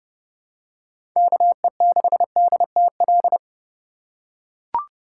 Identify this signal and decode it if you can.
Morse code, heard as a keyed tone, about 30 wpm, 700 Hz. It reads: KE6DTL
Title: USA Call Signs - 2x Character Spacing - 30wpm